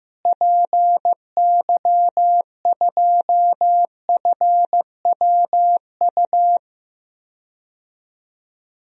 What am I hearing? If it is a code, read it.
Morse code, heard as a keyed tone, about 15 wpm, 700 Hz. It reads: PY2FWU
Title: DXCC-100 Call Signs - 6 Characters - Repeated 2x - 15wpm